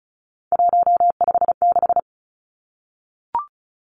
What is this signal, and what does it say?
Morse code, heard as a keyed tone, about 34 wpm, 700 Hz. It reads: 156
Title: Numbers - 3-Digit 35wpm